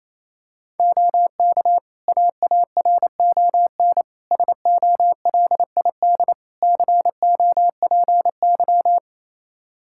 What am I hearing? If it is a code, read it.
Morse code, heard as a keyed tone, about 27 wpm, 700 Hz. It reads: OK AARON SOLID COPY
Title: K1USN SST QSO Practice 28wpm